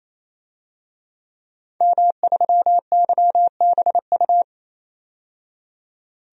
Morse code, heard as a keyed tone, about 28 wpm, 700 Hz. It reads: M3YBU
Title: DXCC-100 Call Signs - 5 Characters - 28wpm